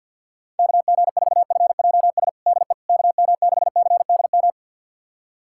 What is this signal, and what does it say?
Morse code, heard as a keyed tone, about 46 wpm, 700 Hz. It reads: KG4FJS DE KM6CDM
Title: USA Callsign DE Callsign 50wpm